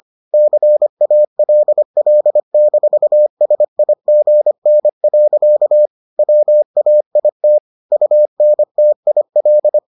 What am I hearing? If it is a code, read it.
Morse code, heard as a keyed tone, about 25 wpm, 600 Hz. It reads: CALL-SIGN. WAIT UNTIL